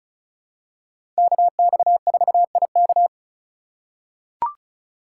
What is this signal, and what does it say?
Morse code, heard as a keyed tone, about 34 wpm, 700 Hz. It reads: KX4IK
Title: USA Call Signs - 5 Characters - 3x Character Spacing - 35wpm